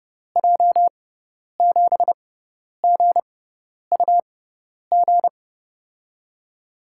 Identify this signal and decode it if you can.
Morse code, heard as a keyed tone, about 30 wpm, spaced as for 10 wpm, 700 Hz. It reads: J7GUG
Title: USA Call Signs - 6x Character Spacing - 30wpm